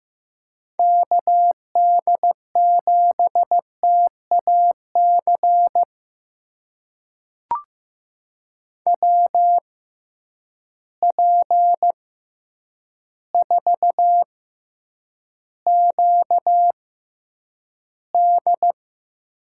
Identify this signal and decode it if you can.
Morse code, heard as a keyed tone, about 15 wpm, spaced as for 8 wpm, 700 Hz. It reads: KD7TAC WP4QD
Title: USA Call Signs - 6 Characters - 6x Character Spacing - 15wpm